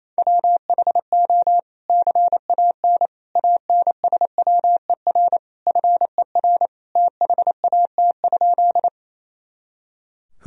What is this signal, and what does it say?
Morse code, heard as a keyed tone, about 28 wpm, 700 Hz. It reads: WHO CAN ANSWER FER THAT?
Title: Sentences from Top 300 Words and Abbreviations 28wpm